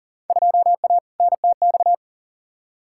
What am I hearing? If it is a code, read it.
Morse code, heard as a keyed tone, about 39 wpm, 700 Hz. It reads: 2A NTX
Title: Field Day Exchanges - 40wpm